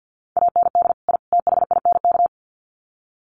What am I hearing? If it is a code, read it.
Morse code, heard as a keyed tone, about 50 wpm, 700 Hz. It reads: AND I THINK
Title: Sets of 3 Words 50wpm